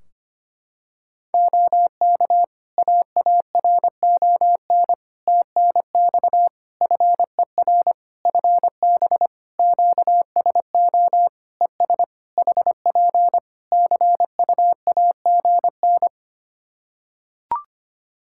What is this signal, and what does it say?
Morse code, heard as a keyed tone, about 25 wpm, 700 Hz. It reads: OK AARON TNX FER FB QSO ES HP CUAGN
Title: K1USN SST QSO Practice 25wpm